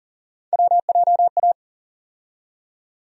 Morse code, heard as a keyed tone, about 40 wpm, 700 Hz. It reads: WJA
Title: USA Call Sign Postfixes - 3 Letters 40wpm